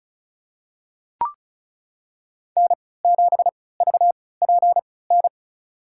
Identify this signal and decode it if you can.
Morse code, heard as a keyed tone, about 34 wpm, spaced as for 20 wpm, 700 Hz. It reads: N7VPN
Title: USA Call Signs - 5 Characters - 3x Character Spacing - 35wpm